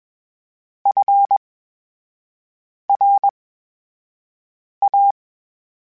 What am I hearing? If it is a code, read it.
Morse code, heard as a keyed tone, about 21 wpm, spaced as for 5 wpm, 800 Hz. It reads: FRA
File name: Special INGs-mp3s / Special INGs-mp3s / VerySlowToFaster Ing @20